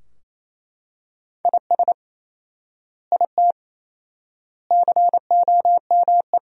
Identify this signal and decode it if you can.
Morse code, heard as a keyed tone, about 28 wpm, 700 Hz. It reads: IS IT COME
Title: 4x Word-Spacing - Sentences from Top 500 Words - 28wpm